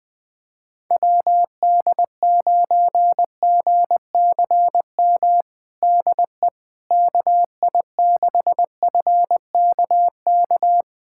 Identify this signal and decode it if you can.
Morse code, heard as a keyed tone, about 20 wpm, 700 Hz. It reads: WD9GCM DE KI6FKK
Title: USA Callsign DE Callsign - 6 Characters - 20wpm